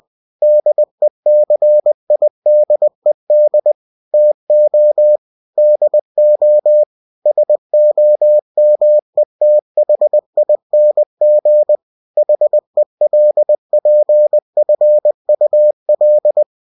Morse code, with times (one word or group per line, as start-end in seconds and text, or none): 0.4-3.7 DECIDED
4.1-5.2 TO
5.6-6.8 DO
7.3-11.8 SOMETHING
12.2-16.4 HELPFUL